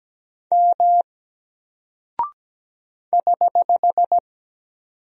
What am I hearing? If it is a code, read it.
Morse code, heard as a keyed tone, about 17 wpm, 700 Hz. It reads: M <HH>
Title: Single Character (Letter:Number:Punctuation:Prosign) Alt 17wpm